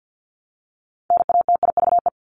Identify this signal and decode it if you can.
Morse code, heard as a keyed tone, about 50 wpm, 700 Hz. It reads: NATIVE
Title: Top 1751-2000 Words 50wpm